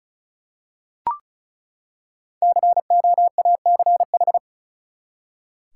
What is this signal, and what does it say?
Morse code, heard as a keyed tone, about 36 wpm, 700 Hz. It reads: COACH
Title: Five Letter Words - 35wpm